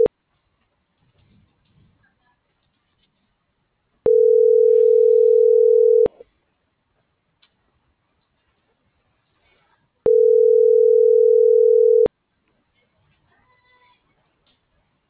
Telephone call progress tone (North America), heard as a telephone network's tone, about 440, 480 Hz. Ringback tone.